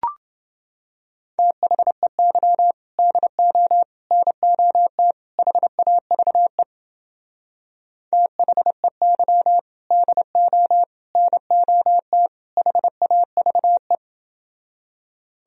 Morse code, heard as a keyed tone, about 30 wpm, 700 Hz. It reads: THEY DO NOT HAVE THEY DO NOT HAVE